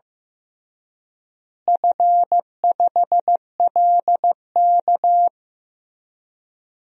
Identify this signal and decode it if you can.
Morse code, heard as a keyed tone, about 15 wpm, 700 Hz. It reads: F5LK
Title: DXCC-100 Call Signs - 4 Characters - Repeated 3x - 15wpm